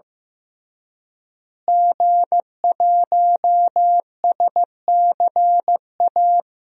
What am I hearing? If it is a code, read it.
Morse code, heard as a keyed tone, about 16 wpm, 700 Hz. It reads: G1SCA
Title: DXCC-100 Call Signs - 5 Characters - Repeated 3x - 15wpm